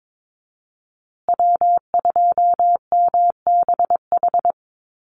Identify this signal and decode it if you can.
Morse code, heard as a keyed tone, about 22 wpm, 700 Hz. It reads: W2MBH
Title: USA Call Signs Repeated 2x 22wpm